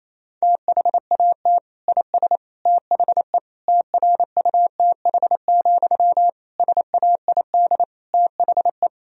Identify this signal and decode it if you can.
Morse code, heard as a keyed tone, about 28 wpm, 700 Hz. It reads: THAT IS THE TRUTH, SAID THE
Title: Sentences from Top 1000 Words - Review Part 1 - 28wpm